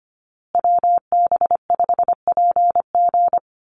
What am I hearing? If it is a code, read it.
Morse code, heard as a keyed tone, about 25 wpm, 700 Hz. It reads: WB5PG
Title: USA Call Signs Repeated 2x 25wpm